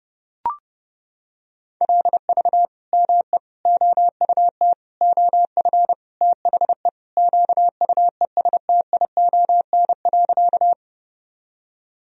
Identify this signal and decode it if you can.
Morse code, heard as a keyed tone, about 30 wpm, 700 Hz. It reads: LV ME OUT OF THE QUESTION.
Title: Sentences from Top 400 Words and Abbreviations 30wpm